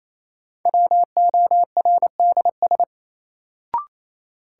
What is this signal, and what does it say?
Morse code, heard as a keyed tone, about 28 wpm, 700 Hz. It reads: WORDS
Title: Sentences from Top 1000 Words - Review Part 6 - New Word Drill - 28wpm